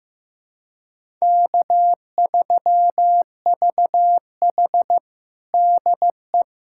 Morse code, heard as a keyed tone, about 16 wpm, 700 Hz. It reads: K3VH DE
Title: USA Callsign DE Callsign - 4 Characters - 15wpm